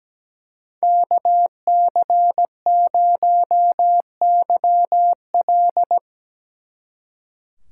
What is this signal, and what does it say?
Morse code, heard as a keyed tone, about 19 wpm, 700 Hz. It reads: KC0YL
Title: Speed Racing - USA 2x2 Call Signs 17wpm